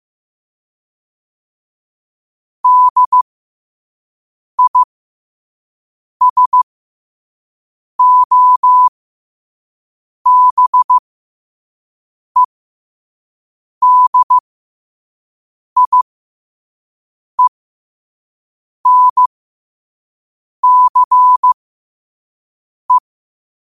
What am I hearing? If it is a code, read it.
Morse code, heard as a keyed tone, about 15 wpm, spaced as for 5 wpm, 1000 Hz. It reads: DISOBEDIENCE
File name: qotdfiles / 5wpm